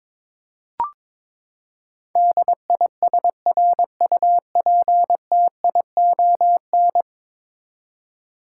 Morse code, heard as a keyed tone, about 22 wpm, 700 Hz. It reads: DISRUPTION
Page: morse-code-ninja-practice-22wpm